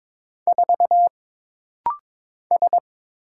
Morse code, heard as a keyed tone, about 22 wpm, 700 Hz. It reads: 4 S
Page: This rapid-fire timing is geared to push your ICR (Instant Character Recognition) skills.